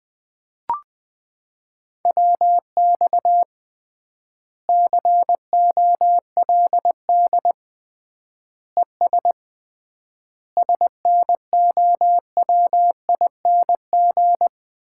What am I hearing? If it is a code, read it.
Morse code, heard as a keyed tone, about 20 wpm, 700 Hz. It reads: WX COLD ES SNOWING
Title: Common QSO Phrases - 3x Word-Spacing - 20wpm